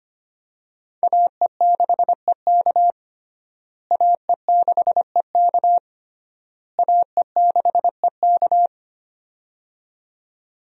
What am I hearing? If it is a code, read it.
Morse code, heard as a keyed tone, about 25 wpm, 700 Hz. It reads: AE6EK AE6EK AE6EK
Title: USA Call Signs - 5 Characters - Repeated 3x - 25wpm